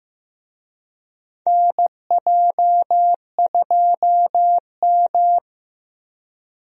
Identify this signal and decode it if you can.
Morse code, heard as a keyed tone, about 15 wpm, 700 Hz. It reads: NJ2M
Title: USA Call Signs - 4 Characters - Repeated 2x - 15wpm